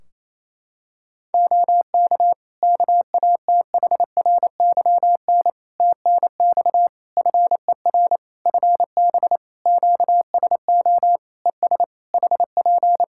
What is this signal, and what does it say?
Morse code, heard as a keyed tone, about 27 wpm, 700 Hz. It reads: OK KATHRYN TNX FER FB QSO ES HP
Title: Practice QSO - Exchange 4 - Ending - 28wpm